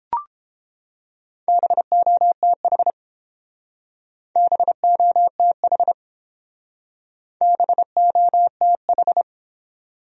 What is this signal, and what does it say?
Morse code, heard as a keyed tone, about 30 wpm, 700 Hz. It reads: BOTH BOTH BOTH